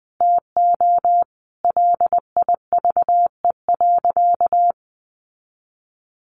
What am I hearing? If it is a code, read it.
Morse code, heard as a keyed tone, about 20 wpm, 700 Hz. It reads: TO LIVE.
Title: Sentences from Top 300 Words 20wpm